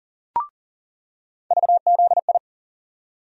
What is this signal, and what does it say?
Morse code, heard as a keyed tone, about 40 wpm, 700 Hz. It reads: VZI
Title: USA Call Sign Postfixes - 3 Letters 40wpm